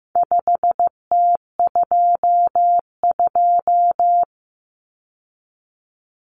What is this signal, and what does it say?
Morse code, heard as a keyed tone, about 15 wpm, 700 Hz. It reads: 5T22
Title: Common Cut Numbers - 4-Digit 15wpm